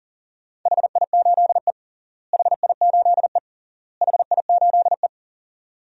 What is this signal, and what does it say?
Morse code, heard as a keyed tone, about 37 wpm, 700 Hz. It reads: HI8E HI8E HI8E
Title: DXCC-100 Call Signs - 4 Characters - Repeated 3x - 40wpm